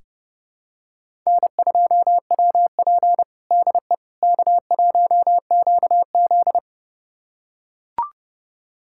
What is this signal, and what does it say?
Morse code, heard as a keyed tone, about 30 wpm, 700 Hz. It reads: N2WP DE K1QZ
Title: USA Callsign DE Callsign - 4 Characters - 30wpm